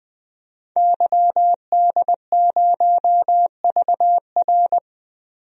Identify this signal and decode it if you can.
Morse code, heard as a keyed tone, about 20 wpm, 700 Hz. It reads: YD0VR
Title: DXCC-100 Call Signs - 5 Characters - 20wpm